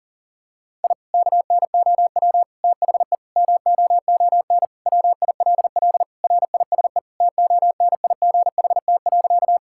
Morse code, heard as a keyed tone, about 40 wpm, 700 Hz. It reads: I KNOW THE MOON WILL RISE TONIGHT.
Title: Sentences from Top 1000 Words - Review Part 6 - 40wpm